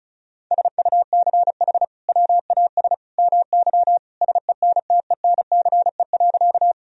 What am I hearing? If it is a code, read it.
Morse code, heard as a keyed tone, about 36 wpm, 700 Hz. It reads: SUCH WAS MY SENTENCE.
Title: Sentences from Top 1000 Words - Review Part 1 - 35wpm